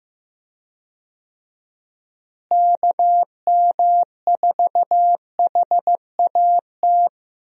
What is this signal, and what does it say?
Morse code, heard as a keyed tone, about 16 wpm, 700 Hz. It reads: KM4HAT
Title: USA Call Signs - 6 Characters - Repeated 2x - 15wpm